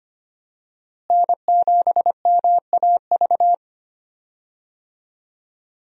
Morse code, heard as a keyed tone, about 25 wpm, 700 Hz. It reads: N7MAV
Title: USA Call Signs - 5 Characters - Repeated 3x - 25wpm